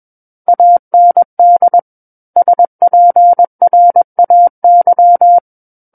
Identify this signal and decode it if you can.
Morse code, heard as a keyed tone, about 21 wpm, 700 Hz. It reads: AND SPRAY
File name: Book One / 12-Chapter 12 - 21 WPM